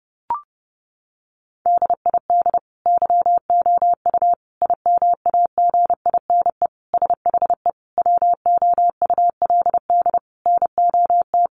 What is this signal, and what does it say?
Morse code, heard as a keyed tone, about 30 wpm, 700 Hz. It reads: DID YOU IMAGINE SHE WOULD NOT
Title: Sentences from Top 1400 Words 30wpm